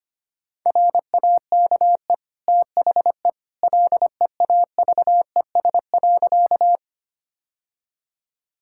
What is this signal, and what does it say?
Morse code, heard as a keyed tone, about 25 wpm, 700 Hz. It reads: RAKE THE LEAVES.
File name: Easy Sentences with Hints - 25wpm